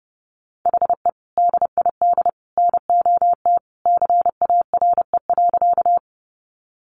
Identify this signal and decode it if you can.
Morse code, heard as a keyed tone, about 30 wpm, 700 Hz. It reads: HE DID NOT CARE.
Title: Sentences from Top 600 Words 30wpm